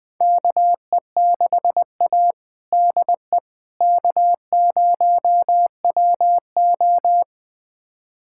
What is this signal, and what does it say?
Morse code, heard as a keyed tone, about 21 wpm, 700 Hz. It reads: KE6A DE K0WO
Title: USA Callsign DE Callsign - 4 Characters - 20wpm